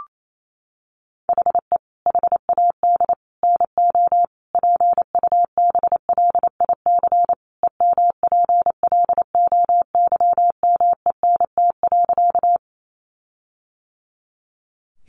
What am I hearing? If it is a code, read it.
Morse code, heard as a keyed tone, about 28 wpm, 700 Hz. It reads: HE HAD NO PUBLIC EMPLOYMENT.
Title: Sentences from Top 1500 Words 28wpm